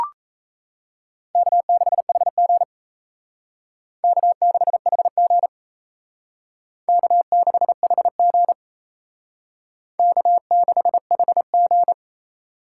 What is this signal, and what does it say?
Morse code, heard as a keyed tone, about 36 wpm, 700 Hz. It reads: K6HG K6HG K6HG K6HG